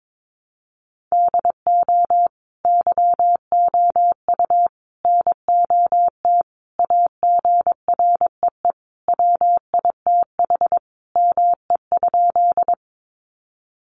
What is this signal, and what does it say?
Morse code, heard as a keyed tone, about 22 wpm, 700 Hz. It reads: DO YOU NOT AGREE WITH ME?
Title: Sentences from Top 1100 Words 22wpm